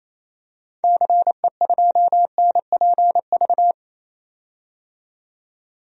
Morse code, heard as a keyed tone, about 28 wpm, 700 Hz. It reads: CE2NPV
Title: DXCC-100 Call Signs - 6 Characters - 28wpm